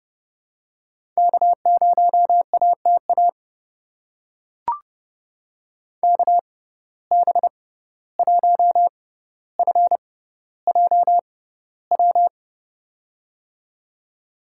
Morse code, heard as a keyed tone, about 30 wpm, spaced as for 15 wpm, 700 Hz. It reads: K0ATA KB1FJW
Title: USA Call Signs - 6x Character Spacing - 30wpm